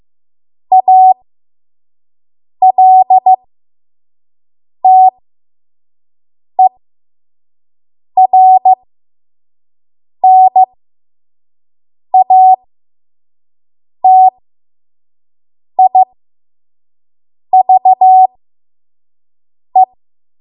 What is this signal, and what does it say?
Morse code, heard as a keyed tone, about 15 wpm, spaced as for 5 wpm, 750 Hz. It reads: ALTERNATIVE
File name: Morse / 05WPM